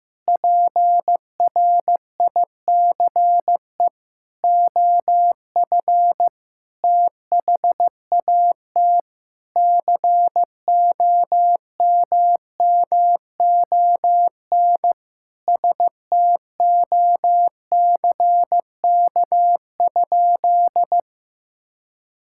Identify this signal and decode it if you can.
Morse code, heard as a keyed tone, about 15 wpm, 700 Hz. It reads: PRICE OF THAT COMMON STOCK?
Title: Sentences from Top 1000 Words - Review Part 5 - 15wpm